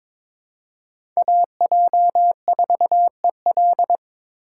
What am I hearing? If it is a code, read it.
Morse code, heard as a keyed tone, about 22 wpm, 700 Hz. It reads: AJ4EL
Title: USA Call Signs - 6x Character Spacing - 22wpm